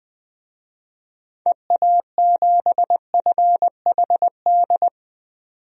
Morse code, heard as a keyed tone, about 20 wpm, 700 Hz. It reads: EA7FHD
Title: DXCC-100 Call Signs - 6 Characters - 20wpm